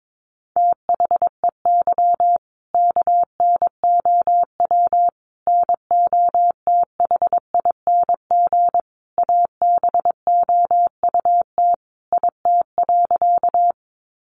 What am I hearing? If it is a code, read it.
Morse code, heard as a keyed tone, about 22 wpm, 700 Hz. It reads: THEY KNOW NOTHING ABOUT IT.
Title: Sentences from Top 500 Words 22wpm